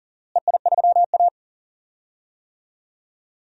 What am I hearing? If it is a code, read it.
Morse code, heard as a keyed tone, about 40 wpm, 700 Hz. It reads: EI3A
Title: DXCC-100 Call Signs - 4 Characters - Repeated 3x - 40wpm